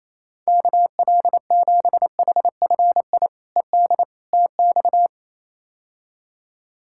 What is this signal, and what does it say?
Morse code, heard as a keyed tone, about 27 wpm, 700 Hz. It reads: KL7HFI ED TX
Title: K1USN SST QSO Practice - 28wpm